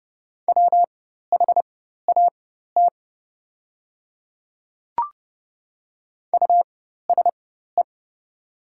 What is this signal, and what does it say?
Morse code, heard as a keyed tone, about 30 wpm, spaced as for 14 wpm, 700 Hz. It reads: WHAT USE
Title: 4x Character Spacing - Top 100 Words - 30wpm